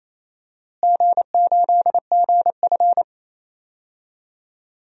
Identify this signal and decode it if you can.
Morse code, heard as a keyed tone, about 28 wpm, 700 Hz. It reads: G8GF
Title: DXCC-100 Call Signs - 4 Characters - Repeated 2x - 28wpm